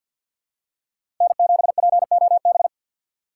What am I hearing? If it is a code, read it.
Morse code, heard as a keyed tone, about 47 wpm, 700 Hz. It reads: N7POB
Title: USA Call Signs - 5 Characters - 3x Character Spacing - 50wpm